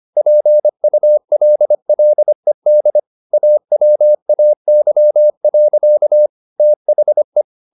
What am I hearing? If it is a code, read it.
Morse code, heard as a keyed tone, about 25 wpm, 600 Hz. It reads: PULLED AWAY. THE